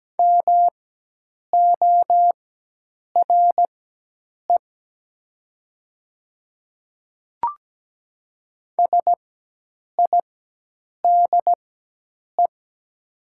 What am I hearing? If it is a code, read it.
Morse code, heard as a keyed tone, about 17 wpm, spaced as for 8 wpm, 700 Hz. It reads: MORE SIDE
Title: 4x Character Spacing - Top 100 Words - 17wpm